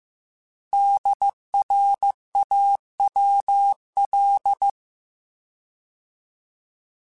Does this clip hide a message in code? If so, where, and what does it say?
Yes: Morse code from 0.7 to 4.7 s, heard as a keyed tone, about 15 wpm, 800 Hz. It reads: DRAWL